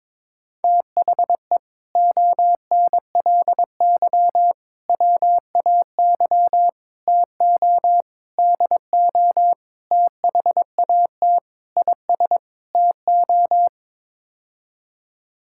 Morse code, heard as a keyed tone, about 23 wpm, 700 Hz. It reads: THE ONLY WAY TO DO THAT IS TO
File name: practice / 22wpm / Sets of 8 Words - 22wpm